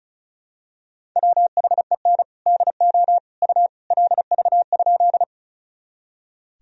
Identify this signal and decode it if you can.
Morse code, heard as a keyed tone, about 34 wpm, 700 Hz. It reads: WHEN DO U LV?
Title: Sentences from Top 400 Words and Abbreviations 35wpm